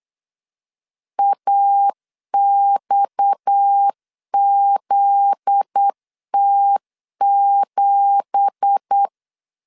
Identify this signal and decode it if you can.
Morse code, heard as a keyed tone, about 8 wpm, 800 Hz. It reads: AXZT7
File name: 8wpm